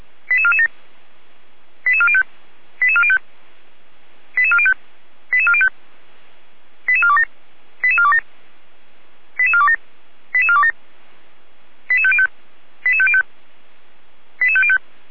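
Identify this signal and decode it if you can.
Five-tone selective call (ZVEI), heard as quick runs of five tones, tones of about 70 ms. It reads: 88447 88405 88405 88405 88405 88427 88427 88427 88427 88505 88505 88505
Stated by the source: ZVEI five-tone selective call 88447 received with an SDR, archived by Feuerwehr Orlamünde (F85360).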